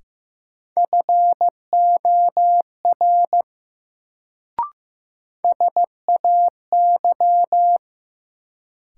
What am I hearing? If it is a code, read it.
Morse code, heard as a keyed tone, about 15 wpm, 700 Hz. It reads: FOR SAY
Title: Sets of 2 Words - New Word Drill - 15wpm